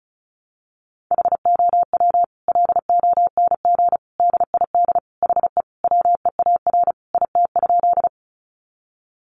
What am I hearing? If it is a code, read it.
Morse code, heard as a keyed tone, about 34 wpm, 700 Hz. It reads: HOW LONG DID HE WEAR IT?
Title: Sentences from Top 1500 Words 35wpm